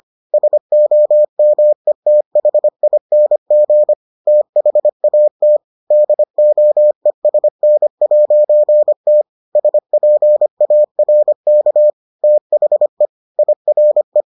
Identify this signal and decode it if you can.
Morse code, heard as a keyed tone, about 25 wpm, 600 Hz. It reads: SOMETHING THAT DOESN'T SPARK THE IRE